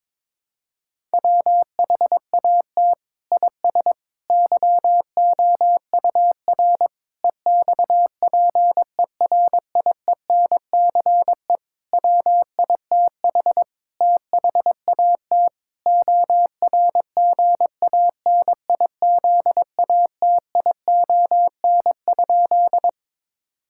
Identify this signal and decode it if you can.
Morse code, heard as a keyed tone, about 22 wpm, 700 Hz. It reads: WHAT IS YOUR EXPERIENCE WITH THAT ORGANIZATION?
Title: Sentences from Top 1000 Words - Review Part 5 - 22wpm